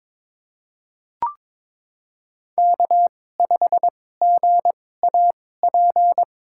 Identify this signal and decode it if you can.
Morse code, heard as a keyed tone, about 22 wpm, spaced as for 16 wpm, 700 Hz. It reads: K5GAP
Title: USA Call Signs - 2x Character Spacing - 22wpm